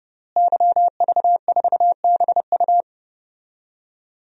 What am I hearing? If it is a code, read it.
Morse code, heard as a keyed tone, about 29 wpm, 700 Hz. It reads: YV4BU
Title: DXCC-100 Call Signs - 5 Characters - Repeated 3x - 30wpm